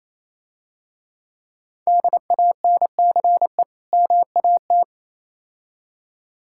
Morse code, heard as a keyed tone, about 27 wpm, 690 Hz. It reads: DANCE MAT
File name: Easy Words - Speed-Racing with Hints - 22wpm